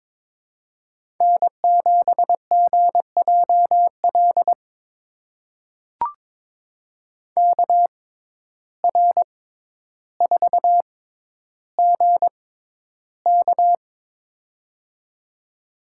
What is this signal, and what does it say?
Morse code, heard as a keyed tone, about 22 wpm, spaced as for 11 wpm, 700 Hz. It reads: N7GJL KR4GK
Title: USA Call Signs - 5 Characters - 6x Character Spacing - 22wpm